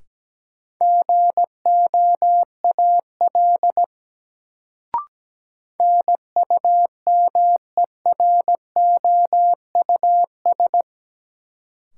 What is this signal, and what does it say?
Morse code, heard as a keyed tone, about 17 wpm, 700 Hz. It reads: GOAL NUMEROUS
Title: Sentences from Top 1400 Words - New Word Drill Alt - 17wpm